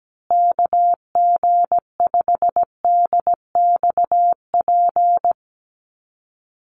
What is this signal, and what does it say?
Morse code, heard as a keyed tone, about 17 wpm, 700 Hz. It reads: KG5DXP
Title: USA Call Signs Repeated 3x 17wpm